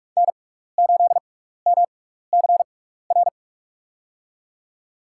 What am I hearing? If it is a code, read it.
Morse code, heard as a keyed tone, about 44 wpm, spaced as for 15 wpm, 700 Hz. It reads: N8MCR
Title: USA Call Signs - 5 Characters - 6x Character Spacing - 45wpm